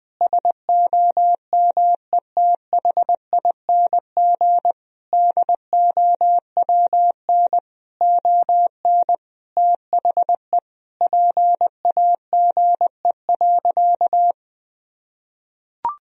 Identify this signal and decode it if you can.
Morse code, heard as a keyed tone, about 20 wpm, 700 Hz. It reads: SOMETHING DOWN ON THE PAGE.
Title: Sentences from Top 1000 Words - Review Part 1 - 20wpm